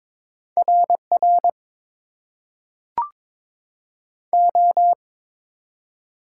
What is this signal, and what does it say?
Morse code, heard as a keyed tone, about 22 wpm, 700 Hz. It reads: RR O
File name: Top 90 QSO Elements - 8x Character Spacing - 22wpm